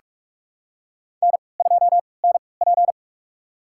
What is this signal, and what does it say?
Morse code, heard as a keyed tone, about 44 wpm, spaced as for 25 wpm, 700 Hz. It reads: N2NP
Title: USA Call Signs - 4 Characters - 3x Character Spacing - 45wpm